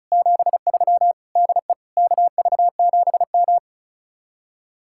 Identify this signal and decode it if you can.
Morse code, heard as a keyed tone, about 44 wpm, 700 Hz. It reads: 73 DE KV7M